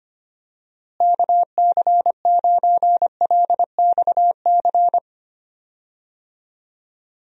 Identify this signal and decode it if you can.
Morse code, heard as a keyed tone, about 25 wpm, 700 Hz. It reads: KC9LXC
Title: USA Call Signs - 6 Characters - Repeated 2x - 25wpm